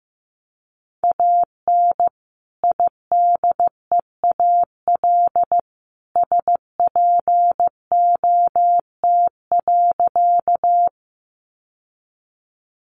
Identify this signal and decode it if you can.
Morse code, heard as a keyed tone, about 15 wpm, 700 Hz. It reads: AN IDEAL SPOT.
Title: Sentences from Top 1400 Words 15wpm